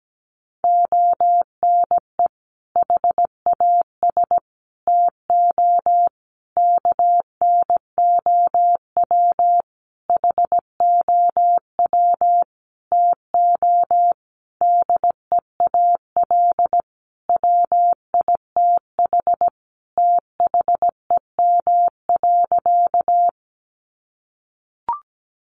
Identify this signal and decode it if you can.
Morse code, heard as a keyed tone, about 17 wpm, 700 Hz. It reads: ONE HAS TO KNOW HOW TO DEAL WITH THEM.
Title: Sentences from Top 800 Words 17wpm